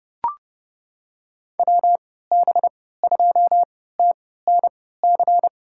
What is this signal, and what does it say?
Morse code, heard as a keyed tone, about 30 wpm, spaced as for 17 wpm, 700 Hz. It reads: WB2TNC
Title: USA Call Signs - 6 Characters - 3x Character Spacing - 30wpm